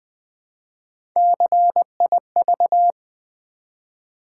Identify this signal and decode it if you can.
Morse code, heard as a keyed tone, about 20 wpm, 700 Hz. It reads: CIV
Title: Build A Word - 1 Letter At A Time - 5-Letter Words - 20wpm